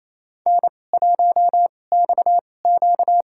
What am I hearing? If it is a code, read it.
Morse code, heard as a keyed tone, about 29 wpm, spaced as for 20 wpm, 700 Hz. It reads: N1XQ
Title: USA Call Signs - 2x Character Spacing - 28wpm